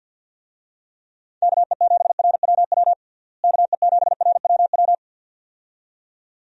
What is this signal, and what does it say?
Morse code, heard as a keyed tone, about 52 wpm, 700 Hz. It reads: KE7RWW KE7RWW
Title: USA Call Signs - 6 Characters - Repeated 2x - 50wpm